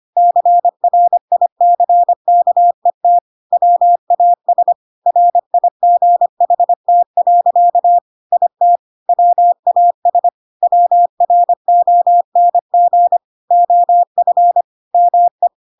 Morse code, heard as a keyed tone, about 25 wpm, 700 Hz. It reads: CRICKET WAS RIGHT. IT WAS WRONG OF ME
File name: Pinocchio / Pinocchio ch5 25 25 ik1hge